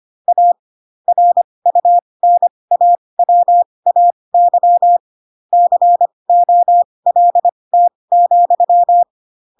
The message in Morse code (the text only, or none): A RUNAWAY COLT,